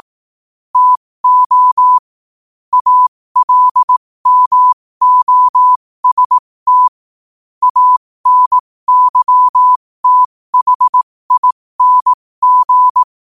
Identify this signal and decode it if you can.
Morse code, heard as a keyed tone, about 18 wpm, spaced as for 15 wpm, 1000 Hz. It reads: TO ALMOST ANYTHING
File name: 15wpm